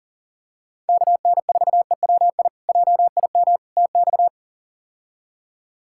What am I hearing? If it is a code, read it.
Morse code, heard as a keyed tone, about 37 wpm, 690 Hz. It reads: KN4EWI JIM TX
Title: K1USN SST QSO Practice - 40wpm